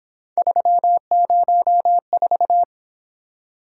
Morse code, heard as a keyed tone, about 27 wpm, 700 Hz. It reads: 304
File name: Speed-Racing - Three Letter-Number Combinations - 22wpm